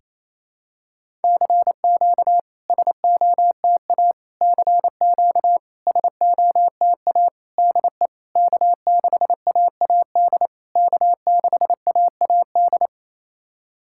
Morse code, heard as a keyed tone, about 27 wpm, 690 Hz. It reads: CQ SOTA CQ SOTA DE K6AAD K6AAD
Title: SOTA QSO Practice 28wpm